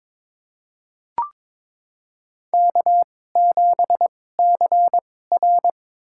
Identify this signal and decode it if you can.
Morse code, heard as a keyed tone, about 22 wpm, spaced as for 16 wpm, 700 Hz. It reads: K7CR
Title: USA Call Signs - 4 Characters - 2x Character Spacing - 22wpm